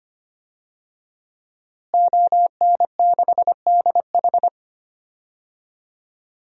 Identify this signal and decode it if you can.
Morse code, heard as a keyed tone, about 25 wpm, 700 Hz. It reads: ON6DH
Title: DXCC-100 Call Signs - 5 Characters - Repeated 3x - 25wpm